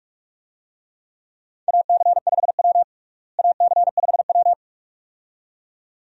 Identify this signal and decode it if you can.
Morse code, heard as a keyed tone, about 42 wpm, 700 Hz. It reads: AC5W AC5W
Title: USA Call Signs - 4 Characters - Repeated 2x - 45wpm